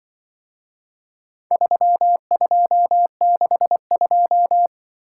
Morse code, heard as a keyed tone, about 24 wpm, 700 Hz. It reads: 3262
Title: Speed-Racing - Four Letter-Number Combinations - 20wpm